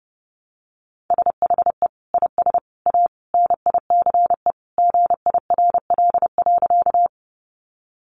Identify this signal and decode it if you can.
Morse code, heard as a keyed tone, about 30 wpm, 700 Hz. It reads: SHE IS A NICE GIRL.
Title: Sentences from Top 1100 Words 30wpm